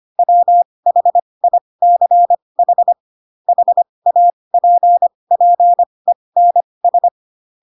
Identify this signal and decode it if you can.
Morse code, heard as a keyed tone, about 25 wpm, spaced as for 20 wpm, 700 Hz. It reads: WHICH HAPPENS
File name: Pinocchio / Pinocchio ch4 15 25 ik1hge